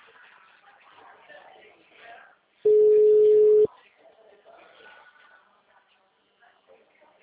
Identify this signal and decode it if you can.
Ringback tone (Europe, 425 Hz), with speech.